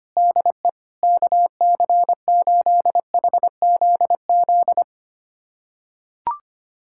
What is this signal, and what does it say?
Morse code, heard as a keyed tone, about 25 wpm, 700 Hz. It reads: DE KC8HZZ
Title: USA Callsign DE Callsign - 6 Characters - 25wpm